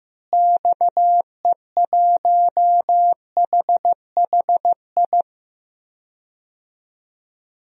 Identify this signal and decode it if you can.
Morse code, heard as a keyed tone, about 15 wpm, 700 Hz. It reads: XE1HHI
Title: DXCC-100 Call Signs - 6 Characters - Repeated 3x - 15wpm